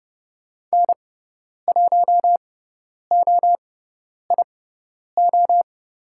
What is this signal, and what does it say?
Morse code, heard as a keyed tone, about 30 wpm, spaced as for 10 wpm, 700 Hz. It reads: N1OIO
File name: Lesson 16g - Letter c - Call Signs 30wpm